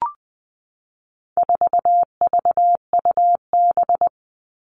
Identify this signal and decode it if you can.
Morse code, heard as a keyed tone, about 20 wpm, 700 Hz. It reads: 4VUB
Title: All Cut Numbers - 4-Digit 20wpm